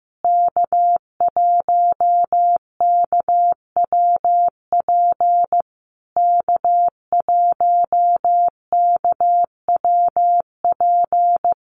Morse code, heard as a keyed tone, about 15 wpm, 700 Hz. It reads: K1KWP K1KWP
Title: USA Call Signs Repeated 3x 15wpm